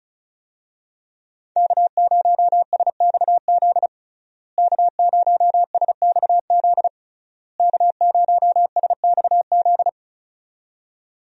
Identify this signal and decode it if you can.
Morse code, heard as a keyed tone, about 36 wpm, 700 Hz. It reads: K0SXZ K0SXZ K0SXZ
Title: USA Call Signs - 5 Characters - Repeated 3x - 35wpm